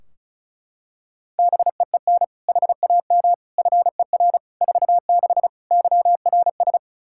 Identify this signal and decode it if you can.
Morse code, heard as a keyed tone, about 34 wpm, 700 Hz. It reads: BEEN HAM FER 46 YRS
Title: Practice QSO - Exchange 3 - More Info - 35wpm